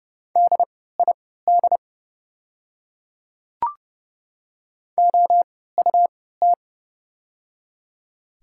Morse code, heard as a keyed tone, about 30 wpm, spaced as for 17 wpm, 690 Hz. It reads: DID OUT
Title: 3x Character Spacing - Top 100 Words - 30wpm